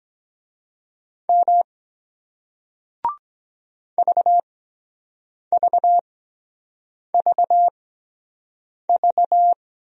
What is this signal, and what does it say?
Morse code, heard as a keyed tone, about 23 wpm, spaced as for 7 wpm, 700 Hz. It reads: M VVVV